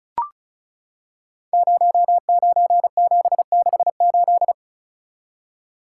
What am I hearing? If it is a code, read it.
Morse code, heard as a keyed tone, about 36 wpm, 700 Hz. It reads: 09768
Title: Code Groups - 5 Letter:Numbers - 35wpm